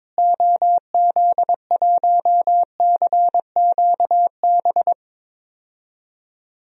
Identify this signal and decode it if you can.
Morse code, heard as a keyed tone, about 21 wpm, 700 Hz. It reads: OZ1CQB
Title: DXCC-100 Call Signs - 6 Characters - Repeated 3x - 22wpm